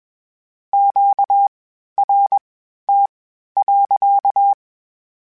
Morse code, heard as a keyed tone, about 21 wpm, spaced as for 12 wpm, 800 Hz. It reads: QRT.
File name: Q Calls / Q Calls-mp3s / Q Calls sppc @20